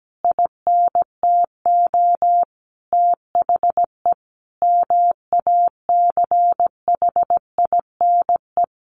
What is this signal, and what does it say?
Morse code, heard as a keyed tone, about 17 wpm, 700 Hz. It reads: INTO THE MACHINE
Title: A Sound Of Thunder 17wpm